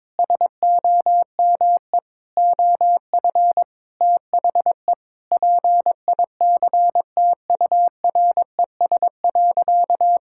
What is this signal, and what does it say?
Morse code, heard as a keyed tone, about 22 wpm, 700 Hz. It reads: SOME OF THE PICTURES.
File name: practice / 22wpm / Sentences with 8-Word Phrases - 22wpm